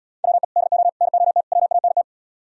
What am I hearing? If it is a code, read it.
Morse code, heard as a keyed tone, about 77 wpm, 700 Hz. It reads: WE DO NOT WANT